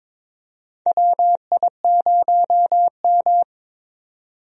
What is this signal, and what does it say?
Morse code, heard as a keyed tone, about 22 wpm, 700 Hz. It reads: WI0M
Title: USA Call Signs - 4 Characters - 22wpm